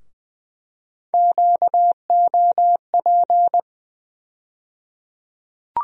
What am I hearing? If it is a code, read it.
Morse code, heard as a keyed tone, about 20 wpm, 700 Hz. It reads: QOP
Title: USA Call Sign Postfixes - 3 Letters 20wpm